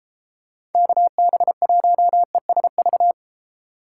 Morse code, heard as a keyed tone, about 30 wpm, 700 Hz. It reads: KB1ESV